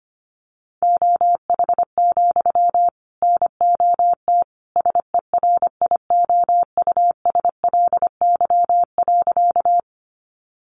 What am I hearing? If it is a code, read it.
Morse code, heard as a keyed tone, about 25 wpm, 700 Hz. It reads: OH, NOT SERIOUSLY.
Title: Sentences from Top 1200 Words 25wpm